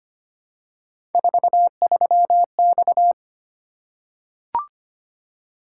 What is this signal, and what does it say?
Morse code, heard as a keyed tone, about 25 wpm, 700 Hz. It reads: 43X
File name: Three Letter-Number Combinations - 25wpm